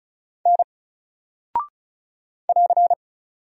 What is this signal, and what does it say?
Morse code, heard as a keyed tone, about 36 wpm, 700 Hz. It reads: N <AR>
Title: Single Character (Letter:Number:Punctuation:Prosign) Alt 35wpm